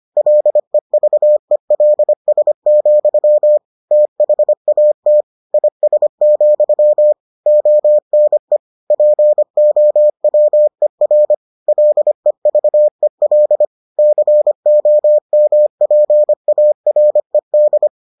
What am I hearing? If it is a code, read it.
Morse code, heard as a keyed tone, about 25 wpm, 600 Hz. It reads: LEVELS, THAT IS, ONE POWER LEVEL COMPARED